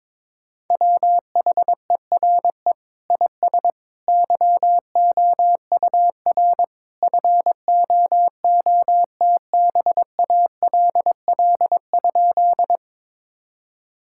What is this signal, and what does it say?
Morse code, heard as a keyed tone, about 22 wpm, 700 Hz. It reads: WHERE IS YOUR FOOTBALL?
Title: Sentences from Top 1600 Words 22wpm